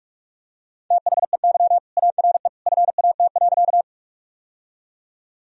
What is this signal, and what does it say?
Morse code, heard as a keyed tone, about 46 wpm, 700 Hz. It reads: THEY ARE FAT.
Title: Sentences from Top 1000 Words - Review Part 4 - 45wpm